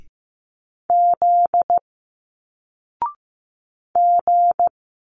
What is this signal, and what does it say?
Morse code, heard as a keyed tone, about 15 wpm, 700 Hz. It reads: Z G
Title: Single Letters 15wpm